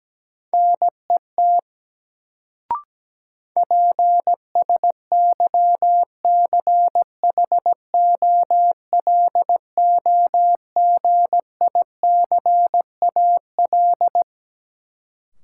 Morse code, heard as a keyed tone, about 17 wpm, 700 Hz. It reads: NET PSYCHOLOGICAL
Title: Sentences from Top 1400 Words - New Word Drill - 17wpm